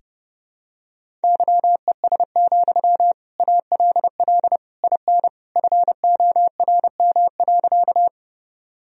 Morse code, heard as a keyed tone, about 32 wpm, 700 Hz. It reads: YES, ALL IN FORM.
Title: Sentences from Top 1000 Words - Review Part 1 - 30wpm